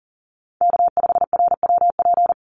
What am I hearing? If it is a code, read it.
Morse code, heard as a keyed tone, about 39 wpm, 700 Hz. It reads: K5RWP
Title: NA SOTA Chaser Callsigns 40wpm